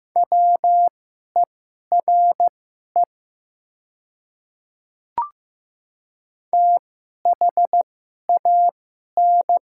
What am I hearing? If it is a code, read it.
Morse code, heard as a keyed tone, about 15 wpm, spaced as for 11 wpm, 700 Hz. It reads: WERE THAN
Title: 2x Character Spacing - Top 300 Words - 15wpm